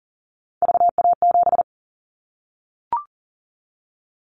Morse code, heard as a keyed tone, about 39 wpm, 700 Hz. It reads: VA7
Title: Callsign Prefixes for Top 100 DXCC - 40wpm